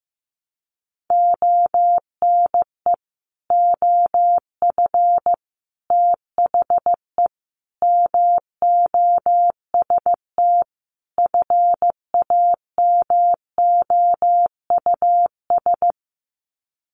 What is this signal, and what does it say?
Morse code, heard as a keyed tone, about 15 wpm, 700 Hz. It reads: ONE OF THE MOST FAMOUS
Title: Sets of 5 Words Encore2 15wpm